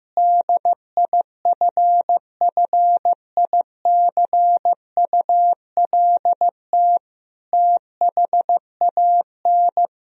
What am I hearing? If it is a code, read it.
Morse code, heard as a keyed tone, about 16 wpm, 700 Hz. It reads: DIFFICULT THAN
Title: Sentences from Top 1000 Words - Review Part 2 - 15wpm